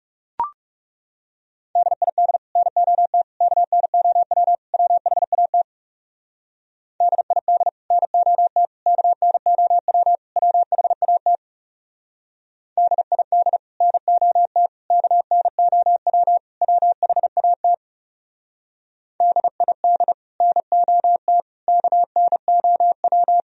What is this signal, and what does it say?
Morse code, heard as a keyed tone, about 35 wpm, 700 Hz. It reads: DID NOT KNOW WHAT DID NOT KNOW WHAT DID NOT KNOW WHAT DID NOT KNOW
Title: Speed-Racing - Sets of 4 Words 30wpm